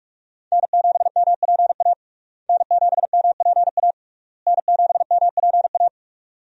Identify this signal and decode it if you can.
Morse code, heard as a keyed tone, about 41 wpm, 700 Hz. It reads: N7MPA N7MPA N7MPA
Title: USA Call Signs - 5 Characters - Repeated 3x - 45wpm